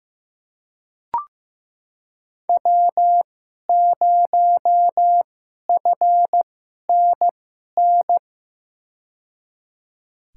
Morse code, heard as a keyed tone, about 15 wpm, spaced as for 11 wpm, 700 Hz. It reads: W0FNN
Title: USA Call Signs - 5 Characters - 2x Character Spacing - 15wpm